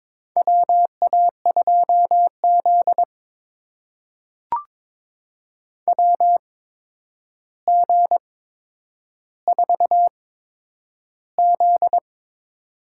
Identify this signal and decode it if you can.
Morse code, heard as a keyed tone, about 22 wpm, spaced as for 9 wpm, 700 Hz. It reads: WA2Z WG4Z
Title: USA Call Signs - 4 Characters - 8x Character Spacing - 22wpm